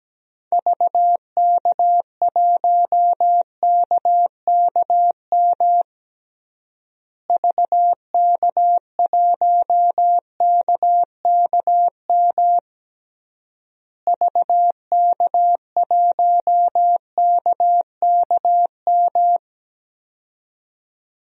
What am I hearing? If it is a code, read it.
Morse code, heard as a keyed tone, about 17 wpm, 700 Hz. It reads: VK1KKM VK1KKM VK1KKM
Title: DXCC-100 Call Signs - 6 Characters - Repeated 3x - 17wpm